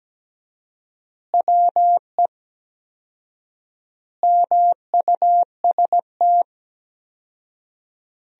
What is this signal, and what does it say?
Morse code, heard as a keyed tone, about 17 wpm, 700 Hz. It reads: WE MUST
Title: 4x Word-Spacing - Sentences from Top 500 Words - 17wpm